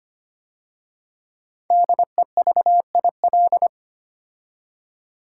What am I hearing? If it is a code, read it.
Morse code, heard as a keyed tone, about 25 wpm, 700 Hz. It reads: DEVIL
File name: 5 Letter Words - First 3 Letters Given - 25wpm